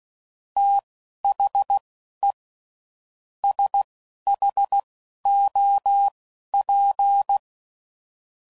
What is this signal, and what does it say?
Morse code, heard as a keyed tone, about 16 wpm, spaced as for 11 wpm, 800 Hz. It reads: THE SHOP
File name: CW Code Groups @10WPM / Weather Sets / Weather 4 28